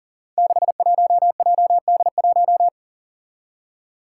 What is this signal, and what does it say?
Morse code, heard as a keyed tone, about 40 wpm, 700 Hz. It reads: 61JD1
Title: Five Challenging Letter-Number Combinations 40wpm